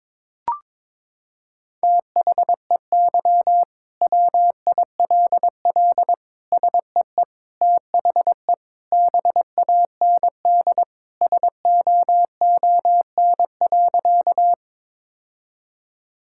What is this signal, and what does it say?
Morse code, heard as a keyed tone, about 22 wpm, 700 Hz. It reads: THEY WILL SEE THE BAND SOON.
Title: Sentences from Top 1600 Words 22wpm